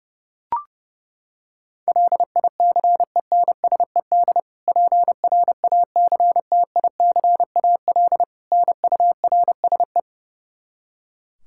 Morse code, heard as a keyed tone, about 32 wpm, 700 Hz. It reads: LICENSED PRACTICAL NURSE
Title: Practice QSO - Exchange 3 - More Info - 30wpm